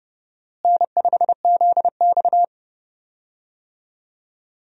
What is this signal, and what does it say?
Morse code, heard as a keyed tone, about 29 wpm, 700 Hz. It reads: N5ZX
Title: USA Call Signs - 4 Characters - Repeated 3x - 30wpm